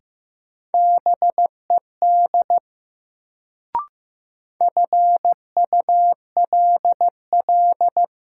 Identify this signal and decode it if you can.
Morse code, heard as a keyed tone, about 15 wpm, 700 Hz. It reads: BED FULL
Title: Sentences from Top 500 Words - New Word Drill - 15wpm